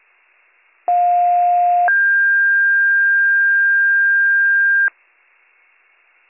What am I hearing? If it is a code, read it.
Two-tone page, heard as a one-second tone, then a three-second tone, about 710 then 1600 Hz. It is heard from 0.9 to 4.9 s.